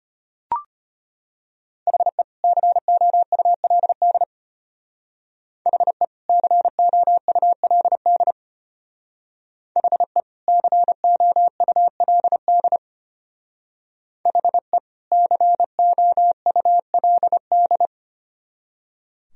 Morse code, heard as a keyed tone, about 31 wpm, 700 Hz. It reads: HE COULD HE COULD HE COULD HE COULD
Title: Speed-Racing - Sets of 2 Words 25wpm